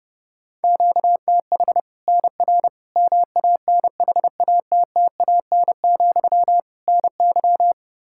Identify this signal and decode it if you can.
Morse code, heard as a keyed tone, about 32 wpm, 700 Hz. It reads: QTH NR MANHATTAN, NY
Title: Practice QSO - Exchange 1 - Beginning - 30wpm